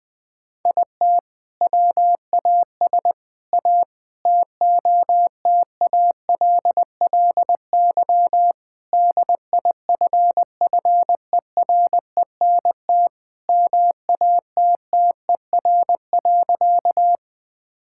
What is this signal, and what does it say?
Morse code, heard as a keyed tone, about 20 wpm, 700 Hz. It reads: IT WAS A TOTALLY DIFFERENT MATTER.
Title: Sentences from Top 1600 Words 20wpm